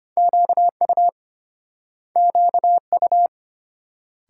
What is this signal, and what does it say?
Morse code, heard as a keyed tone, about 30 wpm, 700 Hz. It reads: QU QU
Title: Speed-Racing Multi-letter Phonemes - 25wpm